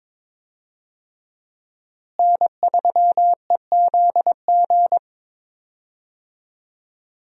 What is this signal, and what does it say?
Morse code, heard as a keyed tone, about 22 wpm, 700 Hz. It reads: N3EZG
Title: USA Call Signs - 5 Characters - Repeated 3x - 22wpm